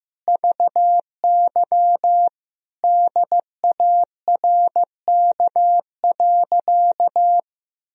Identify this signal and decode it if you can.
Morse code, heard as a keyed tone, about 15 wpm, 700 Hz. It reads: VY DARK.
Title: Sentences from Top 500 Words and Abbreviations 15wpm